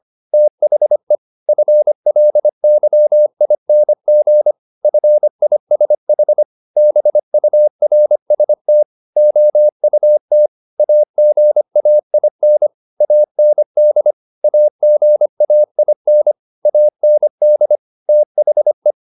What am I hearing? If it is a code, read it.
Morse code, heard as a keyed tone, about 25 wpm, 600 Hz. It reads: THE FLYING FISH BURST OUT AGAIN AND AGAIN AND THE
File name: the old man and the sea part 1 25wpm